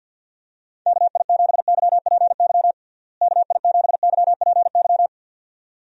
Morse code, heard as a keyed tone, about 52 wpm, 700 Hz. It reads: KI7YPY KI7YPY
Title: USA Call Signs - 6 Characters - Repeated 2x - 50wpm